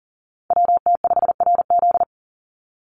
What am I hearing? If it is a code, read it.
Morse code, heard as a keyed tone, about 40 wpm, 700 Hz. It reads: WT5RZ
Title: Call Signs Repeated 3x - 40wpm